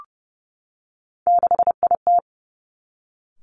Morse code, heard as a keyed tone, about 32 wpm, 700 Hz. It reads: 6IT
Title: Three Challenging Letter-Number Combinations 30wpm